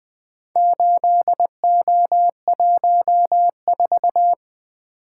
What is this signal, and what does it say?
Morse code, heard as a keyed tone, about 20 wpm, 700 Hz. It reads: 8O14
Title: Four Letter-Number Combinations - 20wpm